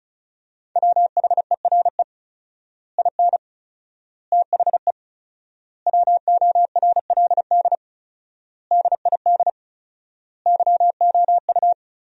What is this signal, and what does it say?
Morse code, heard as a keyed tone, about 35 wpm, 700 Hz. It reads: WHERE IN THE WORLD DID YOU
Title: 4x Word-Spacing - Sentences from Top 200 Words - 35wpm